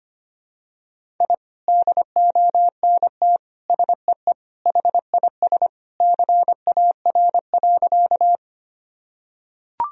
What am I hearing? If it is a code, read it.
Morse code, heard as a keyed tone, about 25 wpm, 700 Hz. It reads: I DONT SEE HIS CAR.
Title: Sentences from Top 1000 Words - Review Part 1 - 25wpm